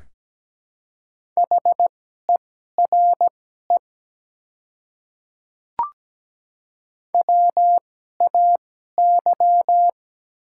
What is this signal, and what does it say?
Morse code, heard as a keyed tone, about 17 wpm, spaced as for 12 wpm, 700 Hz. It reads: HERE WAY
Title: 2x Character Spacing - Top 300 Words - 17wpm